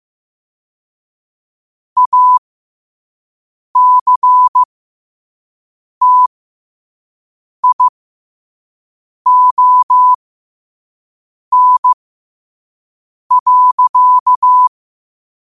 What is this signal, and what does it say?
Morse code, heard as a keyed tone, about 15 wpm, spaced as for 5 wpm, 1000 Hz. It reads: ACTION.
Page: New quotes every day in morse code at 5 Words per minute.